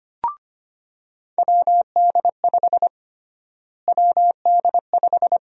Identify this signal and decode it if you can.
Morse code, heard as a keyed tone, about 25 wpm, 700 Hz. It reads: WD5 WD5
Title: USA Call Sign Prefixes - Repeated 2x - 25wpm